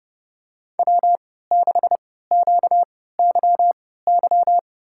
Morse code, heard as a keyed tone, about 30 wpm, spaced as for 17 wpm, 700 Hz. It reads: W6QYY
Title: USA Call Signs - 5 Characters - 3x Character Spacing - 30wpm